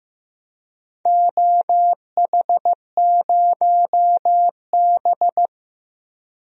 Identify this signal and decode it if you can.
Morse code, heard as a keyed tone, about 15 wpm, 700 Hz. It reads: OH0B
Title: DXCC-100 Call Signs - 4 Characters - Repeated 3x - 15wpm